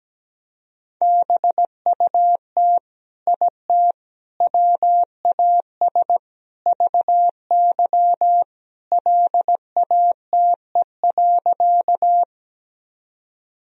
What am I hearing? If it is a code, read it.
Morse code, heard as a keyed tone, about 17 wpm, 700 Hz. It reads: BUT IT WAS VY LATE.
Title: Sentences from Top 300 Words and Abbreviations 17wpm